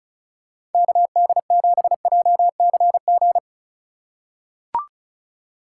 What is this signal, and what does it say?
Morse code, heard as a keyed tone, about 36 wpm, 700 Hz. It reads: KD7JCG
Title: USA Call Signs - 6 Characters - 4x Character Spacing - 35wpm